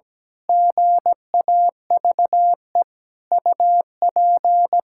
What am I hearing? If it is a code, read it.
Morse code, heard as a keyed tone, about 17 wpm, 700 Hz. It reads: GAVE UP
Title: Pangram Challenge v2 - 17wpm